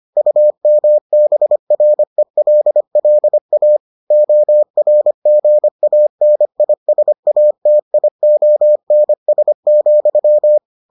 Morse code, heard as a keyed tone, about 25 wpm, 600 Hz. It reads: UMBRELLA ORGANISATIONS,